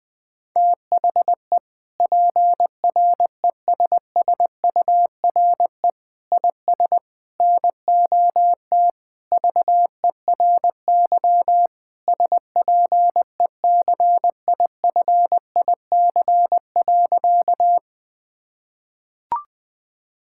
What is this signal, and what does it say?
Morse code, heard as a keyed tone, about 20 wpm, 700 Hz. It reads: THE PRESSURE IS NOT VERY SPECIFIC.
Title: Sentences from Top 1000 Words - Review Part 6 - 20wpm